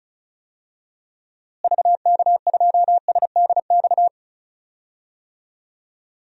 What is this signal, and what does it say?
Morse code, heard as a keyed tone, about 36 wpm, 700 Hz. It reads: VK2SDX
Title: DXCC-100 Call Signs - 6 Characters - Repeated 3x - 35wpm